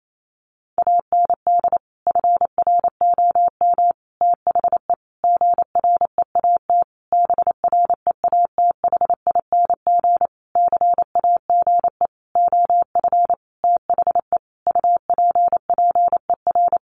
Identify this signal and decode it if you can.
Morse code, heard as a keyed tone, about 28 wpm, 700 Hz. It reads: AND FROM THE GREAT BREATHING CAGE OF THE UPPER
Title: A Sound Of Thunder 28wpm